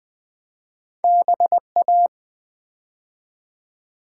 Morse code, heard as a keyed tone, about 20 wpm, 700 Hz. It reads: BA
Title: Build A Word - 2 Letters At A Time - 4-Letter Words - 20wpm